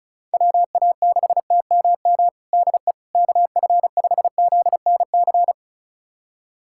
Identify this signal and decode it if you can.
Morse code, heard as a keyed tone, about 34 wpm, 700 Hz. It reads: WA6TMM DE KF5ZNC
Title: USA Callsign DE Callsign - 6 Characters - 35wpm